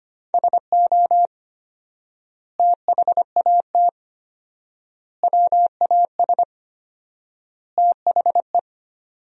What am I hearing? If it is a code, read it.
Morse code, heard as a keyed tone, about 25 wpm, 700 Hz. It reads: SO THAT WAS THE
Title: 4x Word-Spacing - Sentences from Top 100 Words - 25wpm